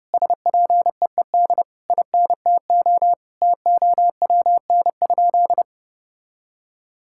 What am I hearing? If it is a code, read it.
Morse code, heard as a keyed tone, about 30 wpm, 700 Hz. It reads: SPEED INTO TOWN?
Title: Sentences from Top 1000 Words - Review Part 5 - 30wpm